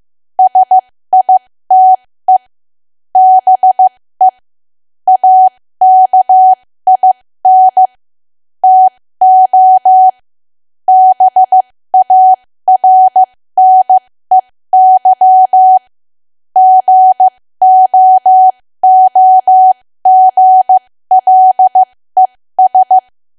Morse code, heard as a keyed tone, about 15 wpm, spaced as for 13 wpm, 750 Hz. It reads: SITE BE AKIN TO BARNEY GOOGLES
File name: cwa / W1AW-Level2 / 13WPM